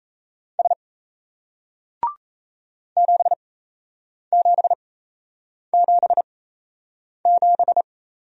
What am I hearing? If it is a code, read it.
Morse code, heard as a keyed tone, about 37 wpm, spaced as for 8 wpm, 700 Hz. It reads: S 7777